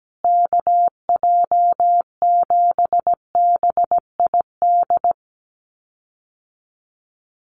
Morse code, heard as a keyed tone, about 17 wpm, 700 Hz. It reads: KJ7BID
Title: USA Call Signs Repeated 3x 17wpm